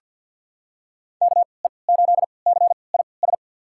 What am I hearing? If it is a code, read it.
Morse code, heard as a keyed tone, about 49 wpm, spaced as for 28 wpm, 700 Hz. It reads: KE8CIS
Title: USA Call Signs - 6 Characters - 3x Character Spacing - 50wpm